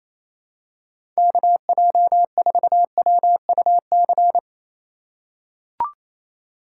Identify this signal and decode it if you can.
Morse code, heard as a keyed tone, about 28 wpm, 700 Hz. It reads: KJ4WUC
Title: USA Call Signs - 8x Character Spacing - 28wpm